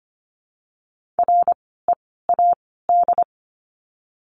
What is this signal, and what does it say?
Morse code, heard as a keyed tone, about 25 wpm, 700 Hz. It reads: R E A D
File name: course / 25:10wpm / 094 Lesson 16f - Letter c - Words 25wpm